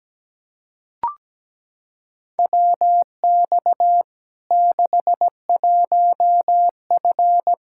Morse code, heard as a keyed tone, about 17 wpm, 700 Hz. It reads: WX 61F
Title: K1USN SST QSO Practice 17wpm